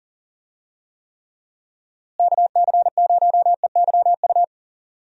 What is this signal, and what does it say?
Morse code, heard as a keyed tone, about 39 wpm, 700 Hz. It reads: KC0EYU
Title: USA Call Signs - 6 Characters - Repeated 3x - 40wpm